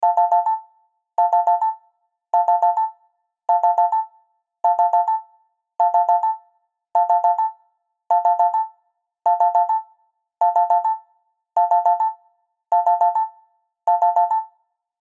There is no code.